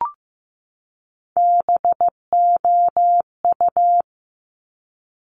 Morse code, heard as a keyed tone, about 15 wpm, 700 Hz. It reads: BOU
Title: Top Three Letter Combinations in Words 15wpm